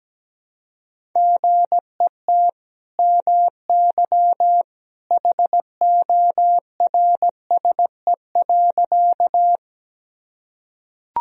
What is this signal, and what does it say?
Morse code, heard as a keyed tone, about 17 wpm, 700 Hz. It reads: GET MY HORSE.
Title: Sentences from Top 1000 Words - Review Part 1 - 17wpm